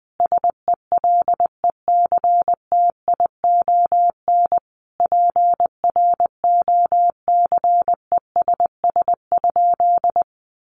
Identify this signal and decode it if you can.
Morse code, heard as a keyed tone, about 20 wpm, 700 Hz. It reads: SELECTION PROCESS?
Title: Sentences from Top 1400 Words 20wpm